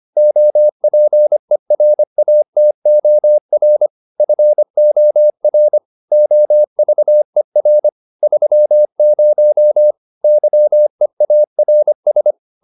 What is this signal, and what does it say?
Morse code, heard as a keyed tone, about 25 wpm, 600 Hz. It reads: OPERATOR FOR OVER 30 YEARS